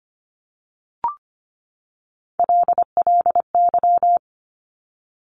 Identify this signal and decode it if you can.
Morse code, heard as a keyed tone, about 25 wpm, 700 Hz. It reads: LLY
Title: Top Three Letter Combinations in Words 25wpm